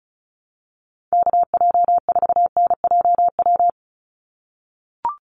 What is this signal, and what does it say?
Morse code, heard as a keyed tone, about 34 wpm, 700 Hz. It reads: KJ4NJW
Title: USA Call Signs 35wpm